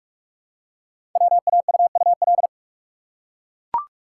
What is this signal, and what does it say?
Morse code, heard as a keyed tone, about 44 wpm, 700 Hz. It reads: WAUUL
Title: Code Groups - 5 Letters - 45wpm